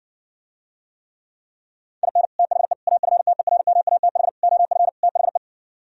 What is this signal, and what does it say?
Morse code, heard as a keyed tone, about 60 wpm, 700 Hz. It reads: IN THE AFTERMATH OF THE